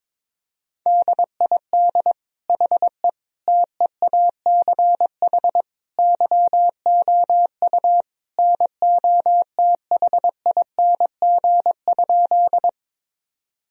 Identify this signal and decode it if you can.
Morse code, heard as a keyed tone, about 22 wpm, 700 Hz. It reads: DID HE TEACH YOU NOTHING?
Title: Sentences from Top 1000 Words - Review Part 2 - 22wpm